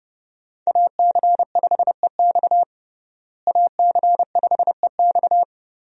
Morse code, heard as a keyed tone, about 29 wpm, 700 Hz. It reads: AC5EX AC5EX
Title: USA Call Signs - 5 Characters - Repeated 2x - 30wpm